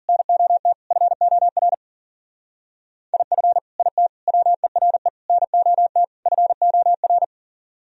Morse code, heard as a keyed tone, about 52 wpm, 700 Hz. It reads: NOT FOR IF IT WERE NOT FOR